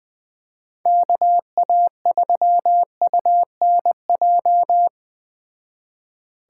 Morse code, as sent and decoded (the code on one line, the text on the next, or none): -.- .- ...-- ..- -. .---
KA3UNJ